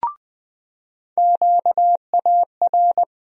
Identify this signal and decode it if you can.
Morse code, heard as a keyed tone, about 20 wpm, 700 Hz. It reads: QAR
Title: USA Call Sign Postfixes - 3 Letters 20wpm